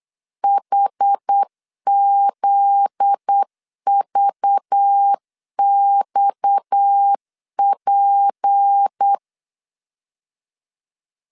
Morse code, heard as a keyed tone, about 8 wpm, 800 Hz. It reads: HZVXP
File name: LEZIONI / 7 GF9Z / 7 GF9Z 8wpm 3213